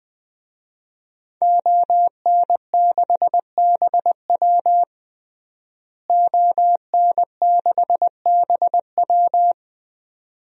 Morse code, heard as a keyed tone, about 20 wpm, 700 Hz. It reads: ON6BW ON6BW
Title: DXCC-100 Call Signs - 5 Characters - Repeated 2x - 20wpm